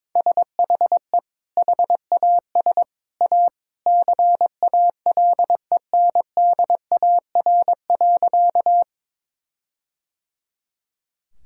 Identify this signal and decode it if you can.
Morse code, heard as a keyed tone, about 22 wpm, 700 Hz. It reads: SHE HAS A CALENDAR.
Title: Sentences from Top 1600 Words 22wpm